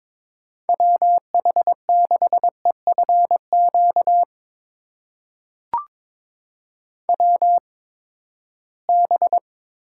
Morse code, heard as a keyed tone, about 21 wpm, spaced as for 15 wpm, 700 Hz. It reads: WH6EFQ WB
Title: USA Call Signs - 8x Character Spacing - 22wpm